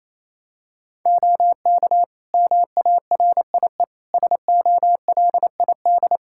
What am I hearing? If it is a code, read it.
Morse code, heard as a keyed tone, about 28 wpm, 700 Hz. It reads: OK MARIE SOLID
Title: Practice QSO - Exchange 3 - More Info - 28wpm